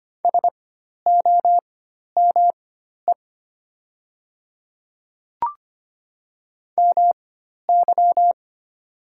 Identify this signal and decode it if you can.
Morse code, heard as a keyed tone, about 25 wpm, spaced as for 12 wpm, 690 Hz. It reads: SOME MY
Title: 4x Character Spacing - Top 200 Words - 25wpm